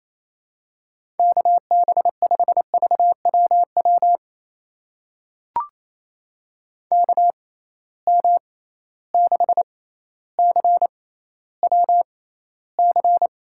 Morse code, heard as a keyed tone, about 27 wpm, spaced as for 14 wpm, 700 Hz. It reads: KB5VWW KM6CWC
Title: USA Call Signs - 6 Characters - 6x Character Spacing - 28wpm